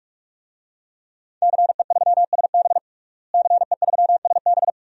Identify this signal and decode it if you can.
Morse code, heard as a keyed tone, about 46 wpm, 700 Hz. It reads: CE3SB CE3SB
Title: DXCC-100 Call Signs - 5 Characters - Repeated 2x - 45wpm